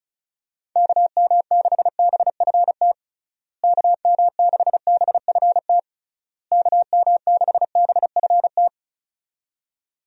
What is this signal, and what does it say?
Morse code, heard as a keyed tone, about 34 wpm, 700 Hz. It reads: KM6BFT KM6BFT KM6BFT
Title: USA Call Signs - 6 Characters - Repeated 3x - 35wpm